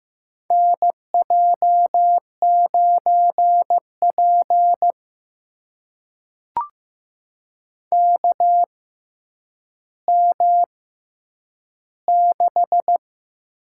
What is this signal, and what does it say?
Morse code, heard as a keyed tone, about 15 wpm, spaced as for 9 wpm, 700 Hz. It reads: NJ9P KM6
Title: USA Call Signs - 6x Character Spacing - 15wpm